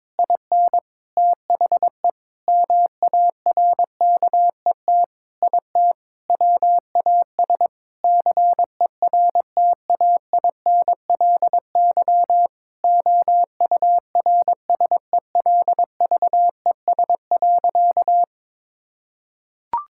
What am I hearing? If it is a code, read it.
Morse code, heard as a keyed tone, about 22 wpm, 700 Hz. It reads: IN THE MARKET IT WAS CERTAINLY OURSELVES.
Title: Sentences from Top 1600 Words 22wpm